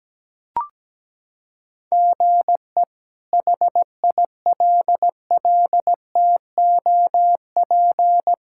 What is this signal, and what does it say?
Morse code, heard as a keyed tone, about 17 wpm, 700 Hz. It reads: GE HILLTOP
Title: K1USN SST QSO Practice - 17wpm